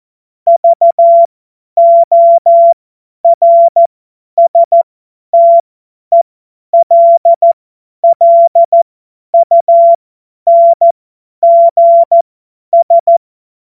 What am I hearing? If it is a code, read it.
Morse code, heard as a keyed tone, about 14 wpm, spaced as for 10 wpm, 680 Hz. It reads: VORSTELLUNGS
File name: Morsekurs / Witze / Witz 17. Januar 2023 50 F70